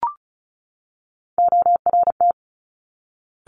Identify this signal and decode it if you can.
Morse code, heard as a keyed tone, about 36 wpm, 700 Hz. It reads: ORT
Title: Top Three Letter Combinations in Words 35wpm